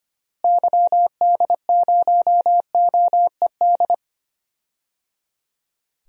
Morse code, heard as a keyed tone, about 25 wpm, 700 Hz. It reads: YD0OED
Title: DXCC-100 Call Signs - 6 Characters - Repeated 3x - 25wpm